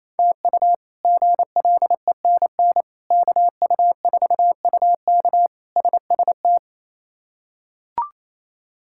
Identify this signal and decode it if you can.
Morse code, heard as a keyed tone, about 28 wpm, 700 Hz. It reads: TU GLENN KU4UK SST
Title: K1USN SST QSO Practice - 28wpm